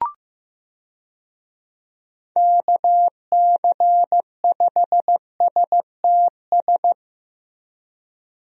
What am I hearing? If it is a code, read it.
Morse code, heard as a keyed tone, about 15 wpm, 700 Hz. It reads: KC5STS
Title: USA Call Signs - 6 Characters - Repeated 3x - 15wpm